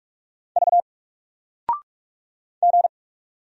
Morse code, heard as a keyed tone, about 46 wpm, 700 Hz. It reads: V G
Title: Single Letter-Number Alt 45wpm